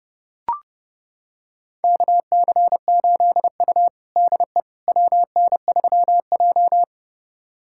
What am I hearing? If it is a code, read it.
Morse code, heard as a keyed tone, about 32 wpm, 700 Hz. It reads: KC8U DE WN3J
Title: USA Callsign DE Callsign - 4 Characters - 30wpm